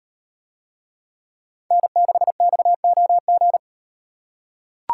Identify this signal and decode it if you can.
Morse code, heard as a keyed tone, about 38 wpm, 700 Hz. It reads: N6XOG